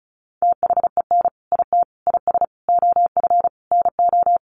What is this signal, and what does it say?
Morse code, heard as a keyed tone, about 34 wpm, 700 Hz. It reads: THEN IT IS OF NO
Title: Sentences from Top 1400 Words 35wpm